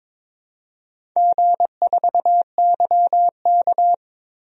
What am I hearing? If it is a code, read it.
Morse code, heard as a keyed tone, about 22 wpm, 700 Hz. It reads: G4YK
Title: DXCC-100 Call Signs - 4 Characters - 22wpm